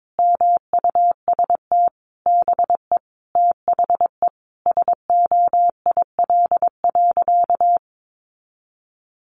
Morse code, heard as a keyed tone, about 22 wpm, 700 Hz. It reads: MUST BE THE SOIL.
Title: Sentences from Top 1500 Words 22wpm